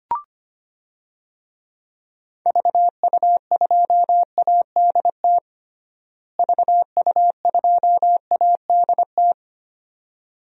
Morse code, heard as a keyed tone, about 25 wpm, 700 Hz. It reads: VU2ADT VU2ADT
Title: DXCC-100 Call Signs - 6 Characters - Repeated 2x - 25wpm